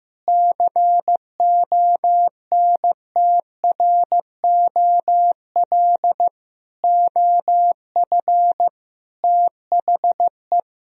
Morse code, heard as a keyed tone, about 15 wpm, 700 Hz. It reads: CONTROL OF THE
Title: Sentences from Top 1000 Words - Review Part 3 - 15wpm